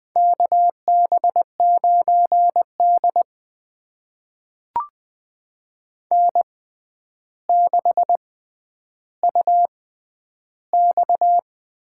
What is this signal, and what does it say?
Morse code, heard as a keyed tone, about 21 wpm, spaced as for 10 wpm, 700 Hz. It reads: KB9D N6UX
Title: USA Call Signs - 4 Characters - 6x Character Spacing - 20wpm